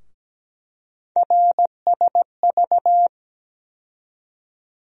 Morse code, heard as a keyed tone, about 17 wpm, 700 Hz. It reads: RSV
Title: USA Call Sign Postfixes - 3 Letters 17wpm